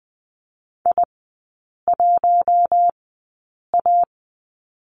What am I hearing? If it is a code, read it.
Morse code, heard as a keyed tone, about 20 wpm, spaced as for 8 wpm, 700 Hz. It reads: I1A
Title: 041 Lesson 8e - Number 1 - Three Characters 20wpm